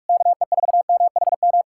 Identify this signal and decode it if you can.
Morse code, heard as a keyed tone, about 69 wpm, spaced as for 56 wpm, 700 Hz. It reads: KE4MHM